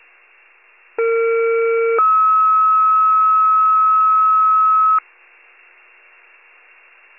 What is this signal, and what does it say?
Two-tone page, heard as a one-second tone, then a three-second tone, about 460 then 1290 Hz. It is heard from 1.0 to 5.0 s.
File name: pager / 21 RESCUE / wcalert